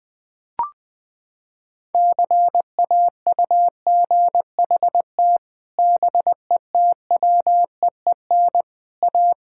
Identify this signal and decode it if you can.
Morse code, heard as a keyed tone, about 20 wpm, 700 Hz. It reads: CAUGHT BETWEEN A
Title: Sets of 8 Words - 20wpm